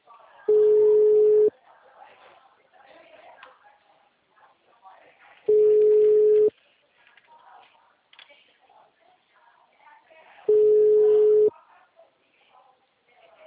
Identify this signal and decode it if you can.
Ringback tone (Europe, 425 Hz), with speech.